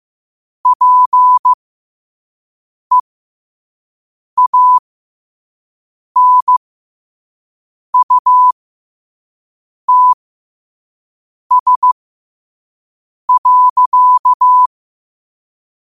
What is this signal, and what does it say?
Morse code, heard as a keyed tone, about 15 wpm, spaced as for 5 wpm, 1000 Hz. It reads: PEANUTS.
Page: New quotes every day in morse code at 5 Words per minute.